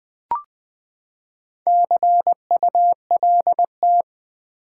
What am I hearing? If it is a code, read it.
Morse code, heard as a keyed tone, about 21 wpm, 700 Hz. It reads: CULT
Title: Build A Word - 4 Letters At A Time - 8-Letter Words - 20wpm